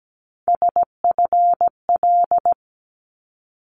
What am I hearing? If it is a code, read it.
Morse code, heard as a keyed tone, about 17 wpm, 700 Hz. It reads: SFL
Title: ARRL Sections 17wpm